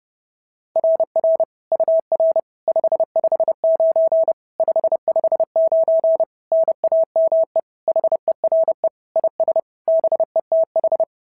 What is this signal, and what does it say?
Morse code, heard as a keyed tone, about 30 wpm, 650 Hz. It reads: RR UR 559 559 NAME HERE IS BETH
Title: SOTA QSO Practice 30wpm